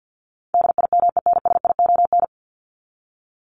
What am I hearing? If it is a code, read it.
Morse code, heard as a keyed tone, about 50 wpm, 700 Hz. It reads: DIMENSION
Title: Top 1501-1750 Words 50wpm